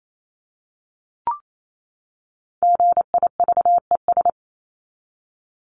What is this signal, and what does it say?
Morse code, heard as a keyed tone, about 28 wpm, 700 Hz. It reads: GIVES
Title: ES:S Suffix - Top 100 Words with Base Words 28wpm